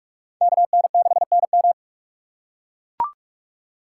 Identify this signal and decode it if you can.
Morse code, heard as a keyed tone, about 42 wpm, 700 Hz. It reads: KN6NM